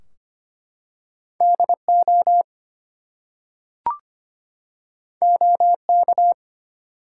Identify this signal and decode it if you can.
Morse code, heard as a keyed tone, about 25 wpm, 700 Hz. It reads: DO OK
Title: Two Letter Words - 25wpm